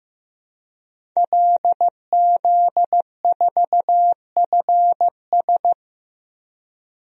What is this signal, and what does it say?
Morse code, heard as a keyed tone, about 15 wpm, 700 Hz. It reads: LZ4FS
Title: DXCC-100 Call Signs - 5 Characters - Repeated 3x - 15wpm